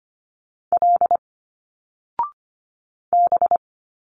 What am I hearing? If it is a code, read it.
Morse code, heard as a keyed tone, about 24 wpm, 700 Hz. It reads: L B
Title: Single Letters 25wpm